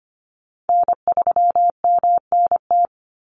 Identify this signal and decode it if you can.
Morse code, heard as a keyed tone, about 25 wpm, 700 Hz. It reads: N3MNT
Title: Call Signs Repeated 2x - 25wpm